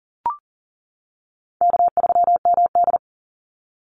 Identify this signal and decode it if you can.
Morse code, heard as a keyed tone, about 40 wpm, 700 Hz. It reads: K3MD
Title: Easy Call Signs - 40wpm